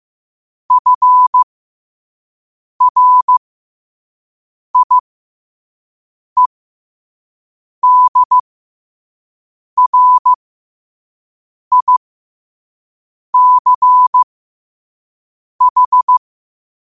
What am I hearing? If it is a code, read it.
Morse code, heard as a keyed tone, about 15 wpm, spaced as for 5 wpm, 1000 Hz. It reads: FRIEDRICH
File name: qotdfiles / 5wpm